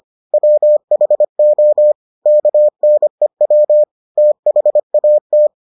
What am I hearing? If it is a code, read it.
Morse code, heard as a keyed tone, about 25 wpm, 600 Hz. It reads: WHO KNEW THAT